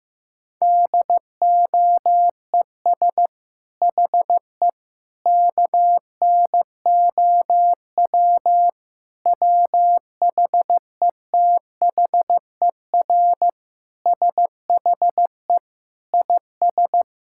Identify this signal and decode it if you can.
Morse code, heard as a keyed tone, about 15 wpm, 700 Hz. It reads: DOES HE KNOW WHETHER SHE IS
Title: Sentences from Top 1000 Words - Review Part 6 - 15wpm